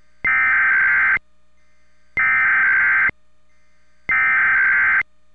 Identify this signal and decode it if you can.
Emergency Alert System (SAME) tones (header bursts), heard as harsh buzzing bursts. It is heard from 0.3 to 5.0 s.